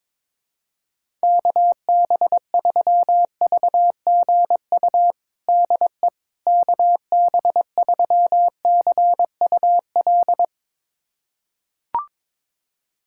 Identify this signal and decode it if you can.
Morse code, heard as a keyed tone, about 22 wpm, 700 Hz. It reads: KB3VGU DE KB3CUL
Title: USA Callsign DE Callsign 22wpm